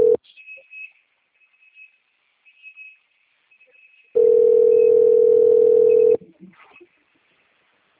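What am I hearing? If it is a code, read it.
Ringback tone (North America), 440 + 480 Hz.
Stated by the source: Outbound telephone call recording from the Itatiba II system.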